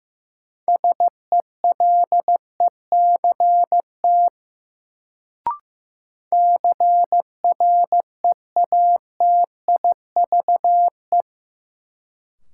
Morse code, heard as a keyed tone, about 15 wpm, 700 Hz. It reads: SELECT CREATIVE
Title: Sentences from Top 1300 Words - New Word Drill Alt - 15wpm